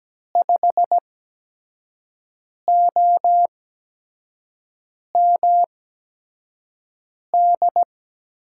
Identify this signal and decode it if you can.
Morse code, heard as a keyed tone, about 17 wpm, spaced as for 5 wpm, 700 Hz. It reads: 5OMD
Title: USA Call Signs - 6 Characters - 8x Character Spacing - 17wpm